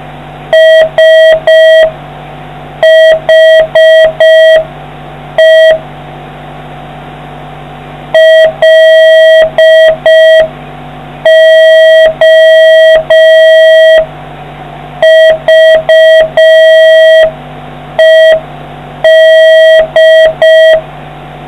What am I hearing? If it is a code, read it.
Morse code, heard as a keyed tone, about 5 wpm, 1950 Hz, with noise. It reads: SHE LOVED